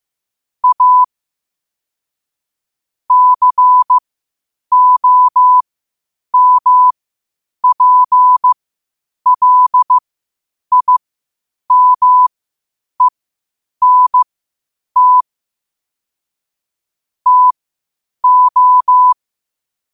Morse code, heard as a keyed tone, about 15 wpm, spaced as for 8 wpm, 1000 Hz. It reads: A COMPLIMENT TO
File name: qotdfiles / 8wpm